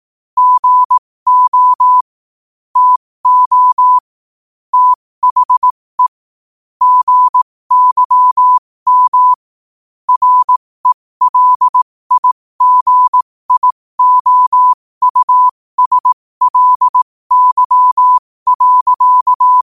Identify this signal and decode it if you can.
Morse code, heard as a keyed tone, about 18 wpm, spaced as for 15 wpm, 1000 Hz. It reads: GO TO THE GYM RELIGIOUSLY.
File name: qotdfiles / 15wpm